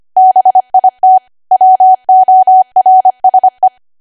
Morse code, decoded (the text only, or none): BIT WORSE